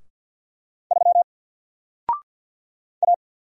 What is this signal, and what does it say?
Morse code, heard as a keyed tone, about 48 wpm, 700 Hz. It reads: 3 A
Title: Single Character (Letter:Number:Punctuation:Prosign) 50wpm